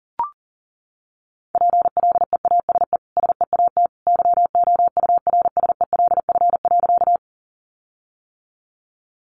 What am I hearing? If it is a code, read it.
Morse code, heard as a keyed tone, about 40 wpm, 700 Hz. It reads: PLEASE SEAT YOURSELF.
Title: Sentences from Top 1500 Words 40wpm